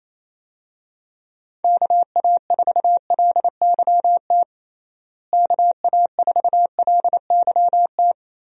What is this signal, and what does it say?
Morse code, heard as a keyed tone, about 27 wpm, 700 Hz. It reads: KA4LYT KA4LYT
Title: USA Call Signs - 6 Characters - Repeated 3x - 28wpm